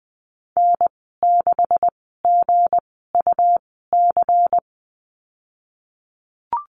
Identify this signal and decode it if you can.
Morse code, heard as a keyed tone, about 20 wpm, spaced as for 15 wpm, 700 Hz. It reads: N6GUC
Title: 193 Lesson 31g - Letter v - Call Signs 20wpm